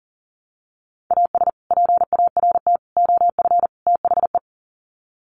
Morse code, heard as a keyed tone, about 40 wpm, 700 Hz. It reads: AS PART OF THE
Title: Sets of 4 Words 40wpm